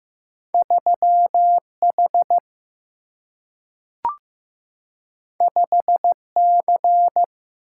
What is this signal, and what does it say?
Morse code, heard as a keyed tone, about 15 wpm, 700 Hz. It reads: 3 H 5 C
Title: Single-Letter Single-Number 15wpm